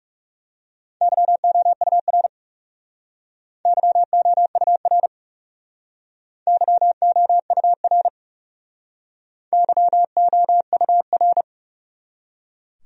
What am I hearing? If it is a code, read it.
Morse code, heard as a keyed tone, about 40 wpm, 700 Hz. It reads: YOUR YOUR YOUR YOUR